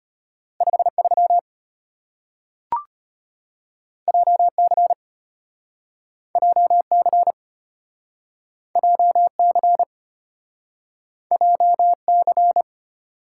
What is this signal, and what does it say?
Morse code, heard as a keyed tone, about 35 wpm, 700 Hz. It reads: 53 JC JC JC JC